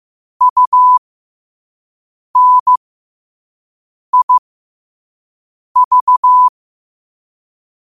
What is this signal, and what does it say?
Morse code, heard as a keyed tone, about 15 wpm, spaced as for 5 wpm, 1000 Hz. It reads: UNIV
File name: qotdfiles / 5wpm